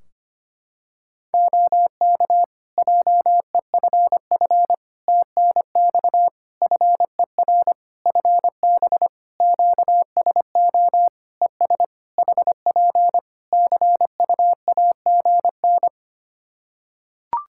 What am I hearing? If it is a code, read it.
Morse code, heard as a keyed tone, about 25 wpm, 700 Hz. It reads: OK JEFF TNX FER FB QSO ES HP CUAGN
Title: K1USN SST QSO Practice 25wpm